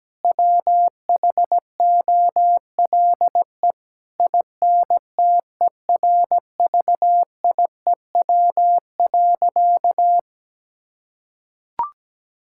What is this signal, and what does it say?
Morse code, heard as a keyed tone, about 17 wpm, 700 Hz. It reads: WHOLE INTERVIEW.
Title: Sentences from Top 1600 Words 17wpm